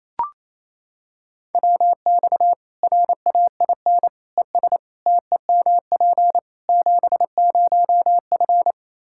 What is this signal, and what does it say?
Morse code, heard as a keyed tone, about 28 wpm, 700 Hz. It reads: WX RAIN ES TEMP 70F
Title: Practice QSO - Exchange 2 - Info - 28wpm